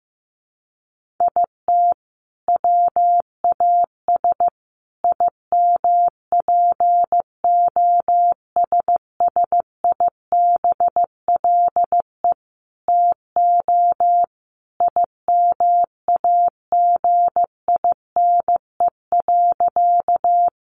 Morse code, heard as a keyed tone, about 15 wpm, 700 Hz. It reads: IT WAS IMPOSSIBLE TO IMAGINE.
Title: Sentences from Top 1400 Words 15wpm